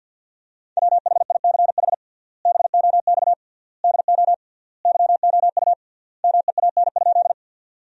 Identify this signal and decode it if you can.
Morse code, heard as a keyed tone, about 50 wpm, 700 Hz. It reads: WHICH BOX DO YOU MEAN?
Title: 3x Word-Spacing - Sentences from Top 500 Words - 50wpm